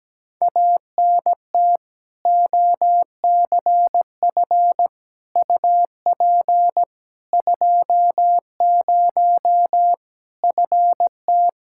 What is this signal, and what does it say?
Morse code, heard as a keyed tone, about 17 wpm, 700 Hz. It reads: ANT OCF UP 20 FT
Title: K1USN SST QSO Practice 17wpm